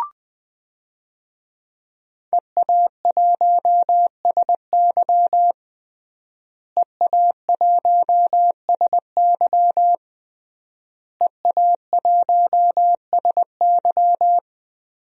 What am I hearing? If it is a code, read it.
Morse code, heard as a keyed tone, about 20 wpm, 700 Hz. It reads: EA1SY EA1SY EA1SY
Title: DXCC-100 Call Signs - 5 Characters - Repeated 3x - 20wpm